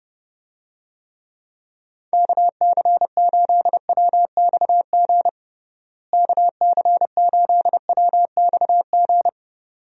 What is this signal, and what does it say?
Morse code, heard as a keyed tone, about 30 wpm, 700 Hz. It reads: KC8WXG KC8WXG
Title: USA Call Signs - 6 Characters - Repeated 2x - 30wpm